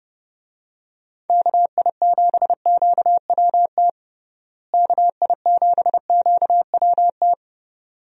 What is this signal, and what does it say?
Morse code, heard as a keyed tone, about 30 wpm, 700 Hz. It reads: KI7QWT KI7QWT
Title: USA Call Signs - 6 Characters - Repeated 2x - 30wpm